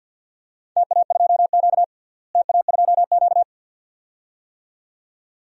Morse code, heard as a keyed tone, about 51 wpm, 700 Hz. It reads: TA2Q TA2Q
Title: DXCC-100 Call Signs - 4 Characters - Repeated 2x - 50wpm